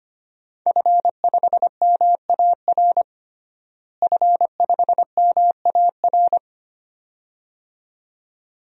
Morse code, heard as a keyed tone, about 25 wpm, 700 Hz. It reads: F5MAR F5MAR
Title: DXCC-100 Call Signs - 5 Characters - Repeated 2x - 25wpm